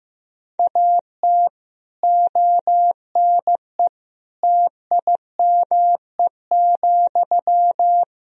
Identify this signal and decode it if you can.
Morse code, heard as a keyed tone, about 15 wpm, 700 Hz. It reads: AT ONE TIME,
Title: Sentences from Top 1000 Words - Review Part 1 - 15wpm